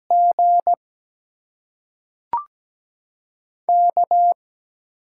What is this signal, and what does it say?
Morse code, heard as a keyed tone, about 17 wpm, spaced as for 3 wpm, 700 Hz. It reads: GK
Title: USA Call Signs - 8x Character Spacing - 17wpm